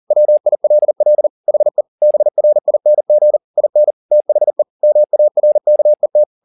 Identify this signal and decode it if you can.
Morse code, heard as a keyed tone, about 40 wpm, 600 Hz. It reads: WILL HE BRING IN THE MARKET